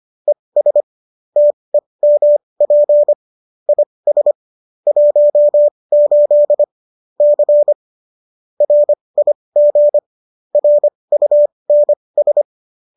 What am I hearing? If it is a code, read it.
Morse code, heard as a keyed tone, about 25 wpm, spaced as for 20 wpm, 590 Hz. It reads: ES TEMP IS 18 C RIG RUNS
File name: qso214 20